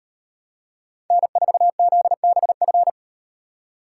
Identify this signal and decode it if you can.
Morse code, heard as a keyed tone, about 38 wpm, 700 Hz. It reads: N4ZBF